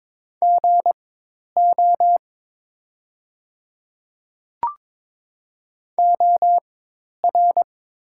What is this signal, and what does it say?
Morse code, heard as a keyed tone, about 22 wpm, spaced as for 10 wpm, 700 Hz. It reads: GO OR
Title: 4x Character Spacing - Top 200 Words - 22wpm